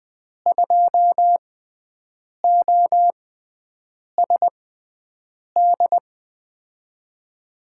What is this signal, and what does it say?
Morse code, heard as a keyed tone, about 20 wpm, spaced as for 7 wpm, 700 Hz. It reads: 2OSD
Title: USA Call Signs - 5 Characters - 6x Character Spacing - 20wpm